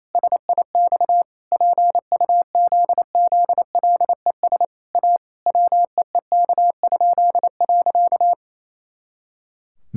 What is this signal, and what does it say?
Morse code, heard as a keyed tone, about 28 wpm, 700 Hz. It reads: SIX PUZZLES A WEEK?.
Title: Pangram Challenge v2 - 28wpm